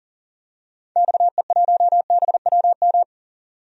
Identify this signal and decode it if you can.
Morse code, heard as a keyed tone, about 40 wpm, 700 Hz. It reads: XE1BWM
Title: DXCC-100 Call Signs - 6 Characters - 40wpm